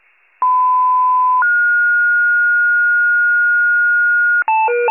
Two-tone page, heard as a one-second tone, then a three-second tone, about 1010 then 1430 Hz. It is heard from 0.4 to 4.4 s.
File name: wcalert